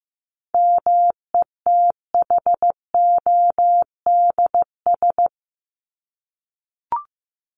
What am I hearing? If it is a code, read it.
Morse code, heard as a keyed tone, about 15 wpm, 700 Hz. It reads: METHODS
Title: ES:S Suffix - Top 100 Words with Base Words 15wpm